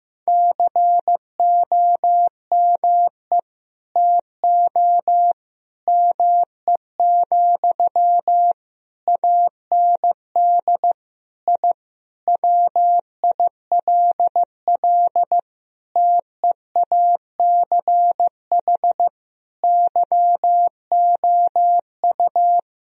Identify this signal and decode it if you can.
Morse code, heard as a keyed tone, about 15 wpm, 700 Hz. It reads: COME TO ME, AND I WILL TEACH YOU
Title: Sentences from Top 1000 Words - Review Part 2 - 15wpm